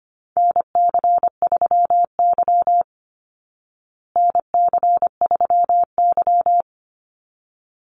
Morse code, heard as a keyed tone, about 25 wpm, 700 Hz. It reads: NC3Y NC3Y
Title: Call Signs Repeated 3x - 25wpm